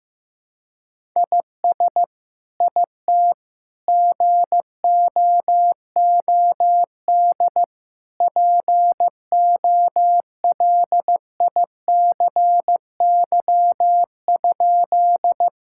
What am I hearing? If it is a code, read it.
Morse code, heard as a keyed tone, about 15 wpm, 700 Hz. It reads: IS IT GOOD POLICY?
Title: Sentences from Top 1000 Words - Review Part 5 - 15wpm